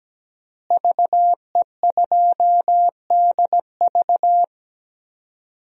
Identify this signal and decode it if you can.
Morse code, heard as a keyed tone, about 17 wpm, 700 Hz. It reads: VE2DV
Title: DXCC-100 Call Signs - 5 Characters - Repeated 2x - 17wpm